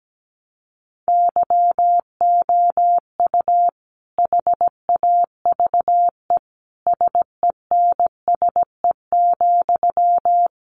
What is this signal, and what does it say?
Morse code, heard as a keyed tone, about 17 wpm, 700 Hz. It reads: YOU HAVE SENSE,
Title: Sentences from Top 1100 Words 17wpm